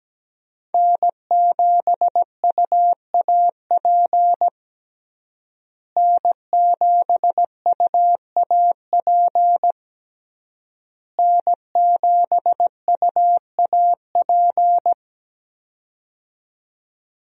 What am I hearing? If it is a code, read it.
Morse code, heard as a keyed tone, about 17 wpm, 700 Hz. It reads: N7UAP N7UAP N7UAP
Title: USA Call Signs - 5 Characters - Repeated 3x - 17wpm